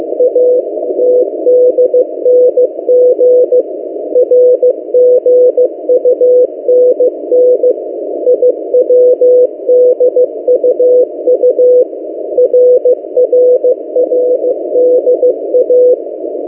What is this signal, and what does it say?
Morse code, heard as a keyed tone, about 15 wpm, 500 Hz. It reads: ATDNG RGUNN IWDUU RRRDA
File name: M08a- -1355utc-8096khz